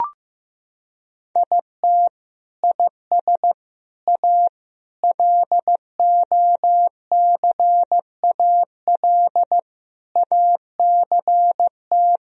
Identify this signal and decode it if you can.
Morse code, heard as a keyed tone, about 16 wpm, 700 Hz. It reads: IT IS A LOCAL ACT
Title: Sentences from Top 1000 Words - Review Part 3 - 15wpm